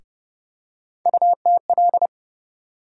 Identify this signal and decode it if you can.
Morse code, heard as a keyed tone, about 30 wpm, 700 Hz. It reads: UTL
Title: USA Call Sign Postfixes - 3 Letters 30wpm